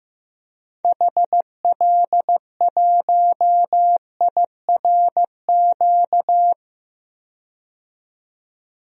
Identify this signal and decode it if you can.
Morse code, heard as a keyed tone, about 15 wpm, 700 Hz. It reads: HL1IRQ
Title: DXCC-100 Call Signs - 6 Characters - 15wpm